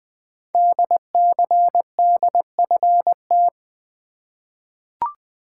Morse code, heard as a keyed tone, about 20 wpm, 700 Hz. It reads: DCDFT
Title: Code Groups - 5 Letters - 20wpm